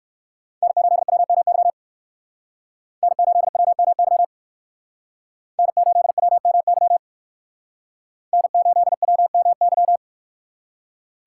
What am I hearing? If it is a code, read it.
Morse code, heard as a keyed tone, about 59 wpm, 700 Hz. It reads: N8WMY N8WMY N8WMY N8WMY